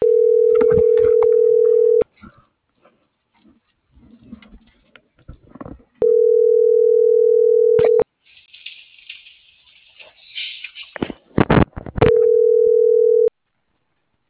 Ringback tone (North America), 440 + 480 Hz.